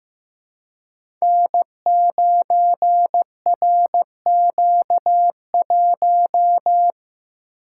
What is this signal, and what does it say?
Morse code, heard as a keyed tone, about 15 wpm, 700 Hz. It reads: N9RQ1
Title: Code Groups - 5 Letter:Numbers - 15wpm